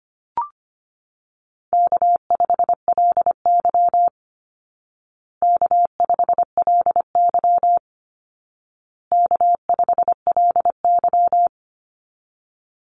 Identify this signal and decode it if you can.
Morse code, heard as a keyed tone, about 25 wpm, 700 Hz. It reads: K5LY K5LY K5LY
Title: Easy Call Signs Repeated 3x - 25wpm